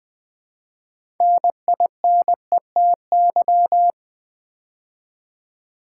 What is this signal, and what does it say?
Morse code, heard as a keyed tone, about 20 wpm, 700 Hz. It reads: NINETY
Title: Build A Word - 2 Letters At A Time - 6-Letter Words - 20wpm